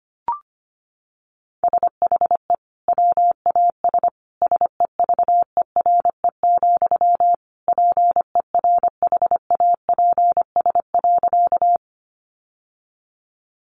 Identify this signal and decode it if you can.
Morse code, heard as a keyed tone, about 25 wpm, 700 Hz. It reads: SHE WAS SEVERE, PERHAPS.
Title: Sentences from Top 1400 Words 25wpm